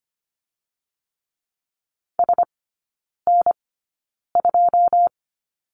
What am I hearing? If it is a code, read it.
Morse code, heard as a keyed tone, about 25 wpm, spaced as for 9 wpm, 700 Hz. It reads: SN2
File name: Lesson 15e - Number 5 - Three Characters 25wpm